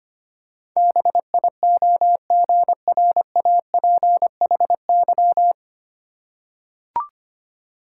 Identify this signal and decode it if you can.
Morse code, heard as a keyed tone, about 25 wpm, 700 Hz. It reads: BIOGRAPHY
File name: practice / 25wpm / Easy Words with Hints - 25wpm